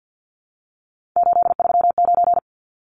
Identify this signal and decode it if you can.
Morse code, heard as a keyed tone, about 50 wpm, 700 Hz. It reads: 839
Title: Numbers - 3-Digit 50wpm